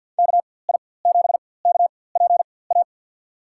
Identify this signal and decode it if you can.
Morse code, heard as a keyed tone, about 50 wpm, spaced as for 23 wpm, 700 Hz. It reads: KI7KPA
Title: USA Call Signs - 6 Characters - 4x Character Spacing - 50wpm